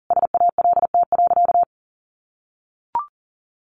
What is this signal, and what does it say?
Morse code, heard as a keyed tone, about 40 wpm, 700 Hz. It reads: SALT.
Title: Sentences from Top 800 Words 40wpm